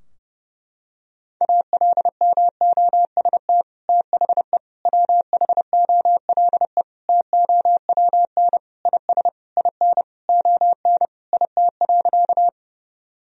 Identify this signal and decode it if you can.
Morse code, heard as a keyed tone, about 32 wpm, 700 Hz. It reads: ALMOST THE WHOLE TOWN IS IN ON IT.
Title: Sentences from Top 1000 Words - Review Part 4 - 30wpm